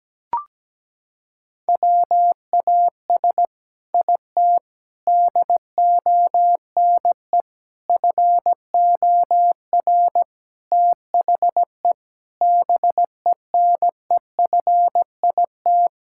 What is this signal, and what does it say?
Morse code, heard as a keyed tone, about 17 wpm, 700 Hz. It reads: WAS IT DONE FOR THE BENEFIT
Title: Sentences from Top 1000 Words - Review Part 6 - 17wpm